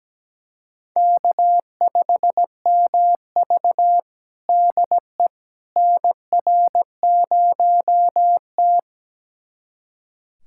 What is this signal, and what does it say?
Morse code, heard as a keyed tone, about 17 wpm, 700 Hz. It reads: K5MV DE NR0T
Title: USA Callsign DE Callsign - 4 Characters - 17wpm